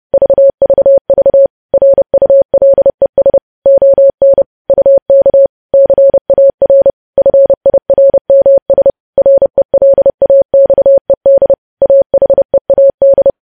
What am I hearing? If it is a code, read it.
Morse code, heard as a keyed tone, about 30 wpm, 550 Hz. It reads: VVV RULES ON UK CAR FIRMS RELAXED AHEAD
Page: news headlines in morse code at 30 wpm